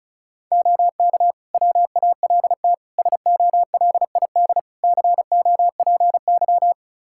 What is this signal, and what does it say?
Morse code, heard as a keyed tone, about 36 wpm, 700 Hz. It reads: OK WALT SOLID COPY
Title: Practice QSO - Exchange 3 - More Info - 35wpm